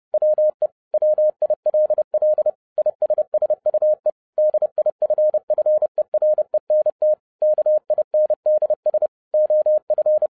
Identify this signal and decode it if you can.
Morse code, heard as a keyed tone, about 30 wpm, 610 Hz. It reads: WE WILL ISSUE DIFFERENT KINDS OF